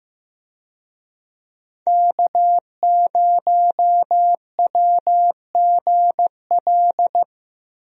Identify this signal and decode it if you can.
Morse code, heard as a keyed tone, about 15 wpm, 700 Hz. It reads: K0WGL
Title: USA Call Signs - 5 Characters - Repeated 3x - 15wpm